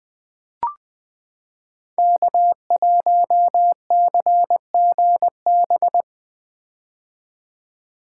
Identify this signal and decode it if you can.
Morse code, heard as a keyed tone, about 21 wpm, 700 Hz. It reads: K1CGB
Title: Code Groups - 5 Letter:Numbers - 20wpm